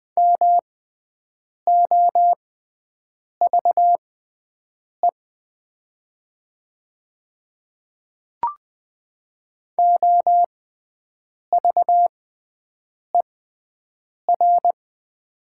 Morse code, heard as a keyed tone, about 20 wpm, spaced as for 7 wpm, 700 Hz. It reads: MOVE OVER
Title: 6x Character Spacing - Top 300 Words - 20wpm